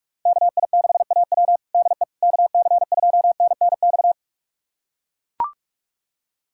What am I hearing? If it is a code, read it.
Morse code, heard as a keyed tone, about 41 wpm, 700 Hz. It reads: KI6AW DE KC2NNX
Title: USA Callsign DE Callsign 45wpm